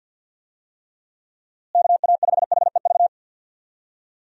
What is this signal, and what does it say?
Morse code, heard as a keyed tone, about 51 wpm, 700 Hz. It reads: KA5HEV
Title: USA Call Signs - 6 Characters - Repeated 3x - 50wpm